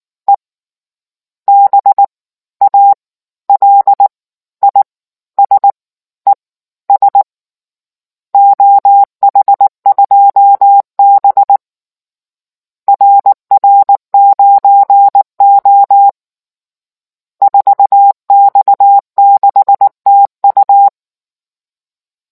Morse code, heard as a keyed tone, about 19 wpm, spaced as for 15 wpm, 800 Hz. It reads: E BALISES OH2B RR9O 4X6TU